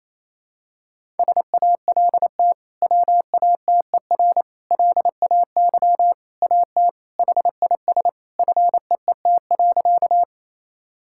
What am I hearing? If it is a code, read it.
Morse code, heard as a keyed tone, about 28 wpm, 700 Hz. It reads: SALT WATER LAY AT HIS FEET.
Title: Sentences from Top 1000 Words - Review Part 5 - 28wpm